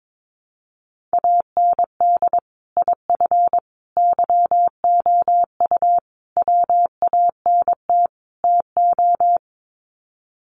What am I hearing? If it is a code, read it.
Morse code, heard as a keyed tone, about 22 wpm, 700 Hz. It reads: AND IF YOU WANT TO
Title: Sets of 5 Words Encore 22wpm